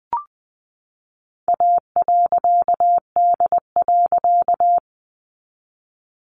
Morse code, heard as a keyed tone, about 20 wpm, 700 Hz. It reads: A.D.
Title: A Sound Of Thunder 20wpm